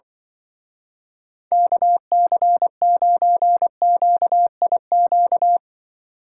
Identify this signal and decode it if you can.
Morse code, heard as a keyed tone, about 24 wpm, 700 Hz. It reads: KC9QIQ
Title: Speed Racing - USA 2x3 Call Signs 20wpm